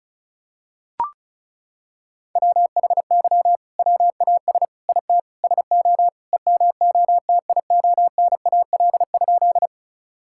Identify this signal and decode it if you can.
Morse code, heard as a keyed tone, about 34 wpm, 700 Hz. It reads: WHY WAS IT SO EMOTIONAL?
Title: Sentences from Top 1000 Words - Review Part 5 - 35wpm